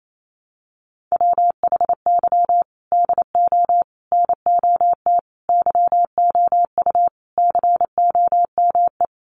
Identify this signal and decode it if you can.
Morse code, heard as a keyed tone, about 28 wpm, 700 Hz. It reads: WHY DO NOT YOU COME
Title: Sets of 5 Words Encore2 28wpm